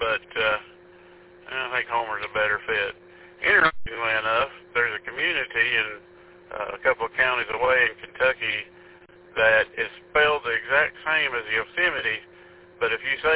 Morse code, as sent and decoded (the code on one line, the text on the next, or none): none